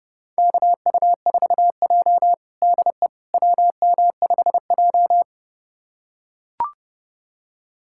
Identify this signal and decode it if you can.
Morse code, heard as a keyed tone, about 30 wpm, 700 Hz. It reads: KU4J DE WM5J
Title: USA Callsign DE Callsign - 4 Characters - 30wpm